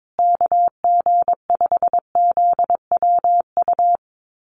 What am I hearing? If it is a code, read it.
Morse code, heard as a keyed tone, about 22 wpm, 700 Hz. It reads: KG5ZWU
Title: USA Call Signs Repeated 3x 22wpm